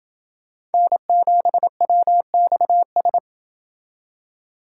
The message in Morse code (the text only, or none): N7WXS